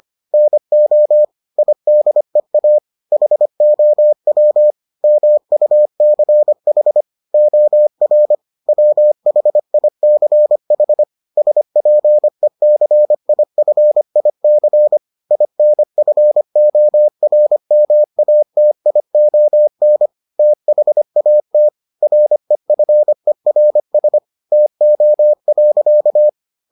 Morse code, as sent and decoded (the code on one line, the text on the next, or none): -. --- / .. -.. .- / .... --- .-- / -- ..- -.-. .... / --- .-. / .-- .... .. -.-. .... / ... .--. -.-. .. ..-. .. -.-. / .. -. ..-. --- .-. -- .- - .. --- -. / - .... .- - / .-. ..-. .-. ... / - --- .-.-.-
NO IDEA HOW MUCH OR WHICH SPECIFIC INFORMATION THAT REFERS TO.